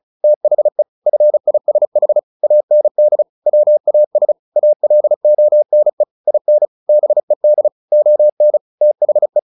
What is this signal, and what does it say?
Morse code, heard as a keyed tone, about 34 wpm, 600 Hz. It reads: THE FISH AND WAS ALONE IN BED ON THE
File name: the old man and the sea 2 35wpm